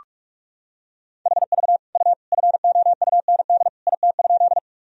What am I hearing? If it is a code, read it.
Morse code, heard as a keyed tone, about 46 wpm, 700 Hz. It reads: HV U FOUND IT?
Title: Sentences from Top 300 Words and Abbreviations 45wpm